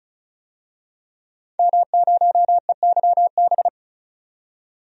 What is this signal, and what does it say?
Morse code, heard as a keyed tone, about 36 wpm, 700 Hz. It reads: M0EYB
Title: DXCC-100 Call Signs - 5 Characters - 35wpm